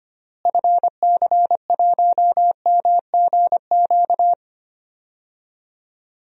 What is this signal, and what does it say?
Morse code, heard as a keyed tone, about 25 wpm, 700 Hz. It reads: FC1MGQ
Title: DXCC-100 Call Signs - 6 Characters - Repeated 2x - 25wpm